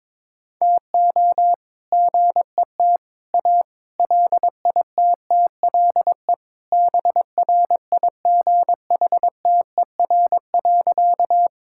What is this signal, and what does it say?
Morse code, heard as a keyed tone, about 22 wpm, 700 Hz. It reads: TO GET A LITTLE BRIGHTER.
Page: morse-code-ninja-practice-22wpm